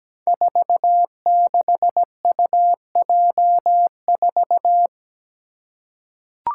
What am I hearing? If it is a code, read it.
Morse code, heard as a keyed tone, about 17 wpm, 700 Hz. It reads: 46UJ4
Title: Five Challenging Letter-Number Combinations 17wpm